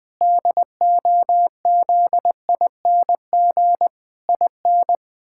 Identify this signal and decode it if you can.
Morse code, heard as a keyed tone, about 20 wpm, 700 Hz. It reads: DOZING IN
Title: Pangram Challenge v2 - 20wpm